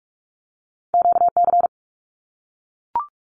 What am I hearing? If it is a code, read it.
Morse code, heard as a keyed tone, about 44 wpm, 700 Hz. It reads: QC
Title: ARRL Sections 45wpm